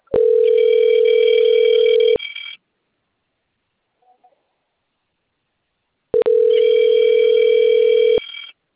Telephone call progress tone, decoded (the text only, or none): ringback tone